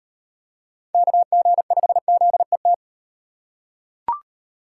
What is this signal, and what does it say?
Morse code, heard as a keyed tone, about 34 wpm, 700 Hz. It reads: KG5ZET